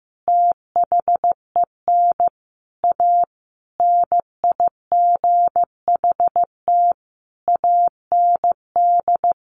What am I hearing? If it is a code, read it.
Morse code, heard as a keyed tone, about 15 wpm, 700 Hz. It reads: THEN A NIGHT AND
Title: A Sound Of Thunder 15wpm